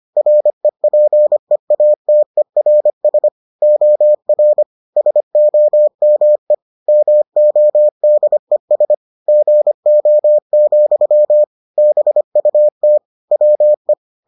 Morse code, heard as a keyed tone, about 25 wpm, 600 Hz. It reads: REPEATERS OR SOME MODES GO, BUT WE